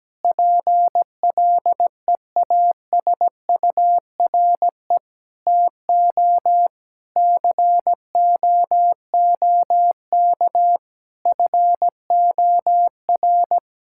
Morse code, heard as a keyed tone, about 17 wpm, 700 Hz. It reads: PLEASURE TO COOK FOR
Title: Sentences from Top 1600 Words 17wpm